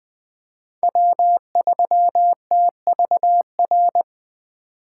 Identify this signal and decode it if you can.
Morse code, heard as a keyed tone, about 20 wpm, 700 Hz. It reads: W3TVR
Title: USA Call Signs - 5 Characters - 2x Character Spacing - 20wpm